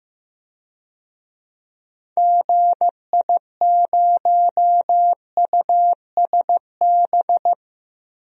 Morse code, heard as a keyed tone, about 16 wpm, 700 Hz. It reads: GI0USB
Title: DXCC-100 Call Signs - 6 Characters - Repeated 3x - 15wpm